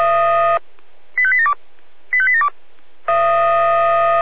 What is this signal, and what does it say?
Five-tone selective call (ZVEI), heard as quick runs of five tones, tones of about 70 ms. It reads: 85872 85872